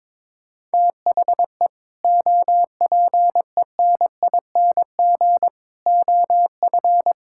Morse code, heard as a keyed tone, about 22 wpm, 700 Hz. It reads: THE OPENING OF
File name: practice / 22wpm / Sentences with 3-Word Phrases - 22wpm